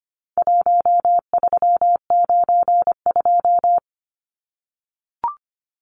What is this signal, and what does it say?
Morse code, heard as a keyed tone, about 25 wpm, 700 Hz. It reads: 1392
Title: Numbers - Contest Style - 4-Digit 25wpm